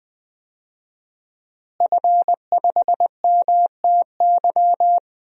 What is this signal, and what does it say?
Morse code, heard as a keyed tone, about 20 wpm, 700 Hz. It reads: F5MTY
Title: DXCC-100 Call Signs - 5 Characters - Repeated 3x - 20wpm